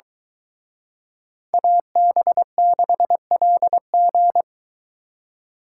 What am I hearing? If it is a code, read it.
Morse code, heard as a keyed tone, about 23 wpm, 700 Hz. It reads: AB6LG